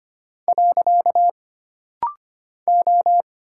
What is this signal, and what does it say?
Morse code, heard as a keyed tone, about 25 wpm, 700 Hz. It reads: . O
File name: Single Character (Letter:Number:Punctuation:Prosign) - Rapid-Fire - 25wpm